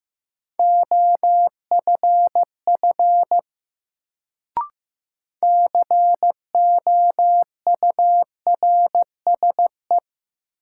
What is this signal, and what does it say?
Morse code, heard as a keyed tone, about 15 wpm, 700 Hz. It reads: OFF COURSE
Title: Sets of 2 Words - New Word Drill - 15wpm